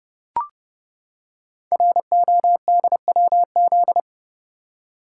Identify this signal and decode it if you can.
Morse code, heard as a keyed tone, about 29 wpm, 700 Hz. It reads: RODWZ
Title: Code Groups - 5 Letters - 30wpm